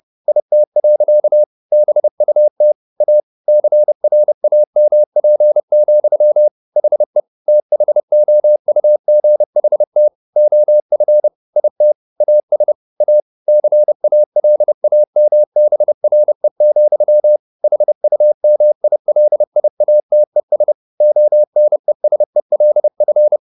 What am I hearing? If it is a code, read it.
Morse code, heard as a keyed tone, about 30 wpm, 600 Hz. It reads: IT. BUT A CRAMP, HE THOUGHT OF IT AS A CALAMBRE, HUMILIATES ONESELF
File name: the old man and the sea / the old man and the sea 1 30wpm